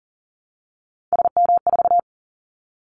Morse code, heard as a keyed tone, about 40 wpm, 700 Hz. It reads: SM4
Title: Callsign Prefixes for Top 25 DXCC - Repeated 3x - 40wpm